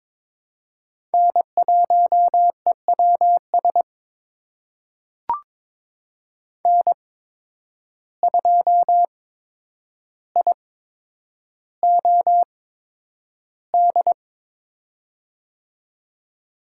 Morse code, heard as a keyed tone, about 22 wpm, spaced as for 9 wpm, 700 Hz. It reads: N1EWS N2IOD
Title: USA Call Signs - 5 Characters - 8x Character Spacing - 22wpm